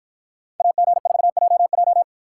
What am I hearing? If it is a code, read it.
Morse code, heard as a keyed tone, about 53 wpm, 700 Hz. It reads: AG4JJ